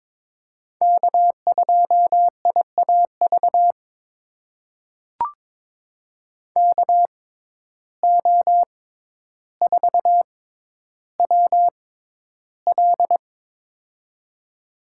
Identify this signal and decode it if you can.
Morse code, heard as a keyed tone, about 22 wpm, spaced as for 11 wpm, 700 Hz. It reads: K2IAV KO4WL
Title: USA Call Signs - 5 Characters - 6x Character Spacing - 22wpm